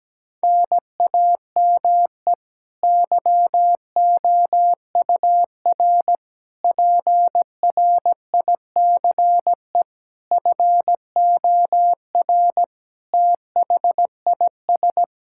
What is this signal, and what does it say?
Morse code, heard as a keyed tone, about 17 wpm, 700 Hz. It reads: NAME YOUR PRICE FOR THIS
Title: Sentences from Top 1000 Words - Review Part 5 - 17wpm